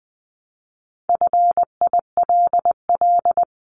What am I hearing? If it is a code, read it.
Morse code, heard as a keyed tone, about 20 wpm, 700 Hz. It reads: FILL
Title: Top 500 Words Repeated 20wpm